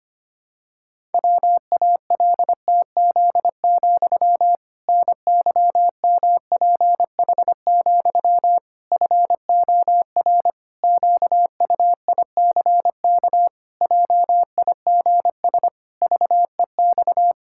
Morse code, heard as a keyed tone, about 25 wpm, 700 Hz. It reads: WALTZ, NYMPH, FOR QUICK JIGS VEX
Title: Pangram Challenge v2 - 25wpm